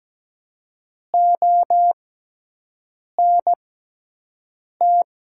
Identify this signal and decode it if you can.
Morse code, heard as a keyed tone, about 17 wpm, spaced as for 6 wpm, 700 Hz. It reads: ONT
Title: 6x Character Spacing - Top 500 Words - 17wpm